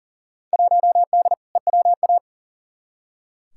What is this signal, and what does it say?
Morse code, heard as a keyed tone, about 40 wpm, 700 Hz. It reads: 1D EWA
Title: Field Day Exchanges - 40wpm